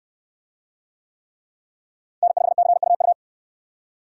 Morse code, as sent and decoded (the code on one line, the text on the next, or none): -. ..... -.-. ..- ..-
N5CUU